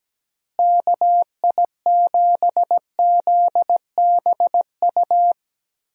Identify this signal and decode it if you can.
Morse code, heard as a keyed tone, about 17 wpm, 700 Hz. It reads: KI7ZBU
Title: USA Call Signs - 6 Characters - Repeated 3x - 17wpm